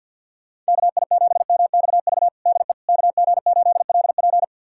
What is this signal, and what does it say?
Morse code, heard as a keyed tone, about 50 wpm, 700 Hz. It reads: KI7MXV DE KG8LP
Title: USA Callsign DE Callsign 50wpm